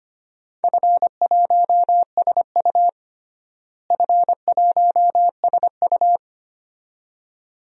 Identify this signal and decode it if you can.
Morse code, heard as a keyed tone, about 25 wpm, 700 Hz. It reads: F1SU F1SU
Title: DXCC-100 Call Signs - 4 Characters - Repeated 2x - 25wpm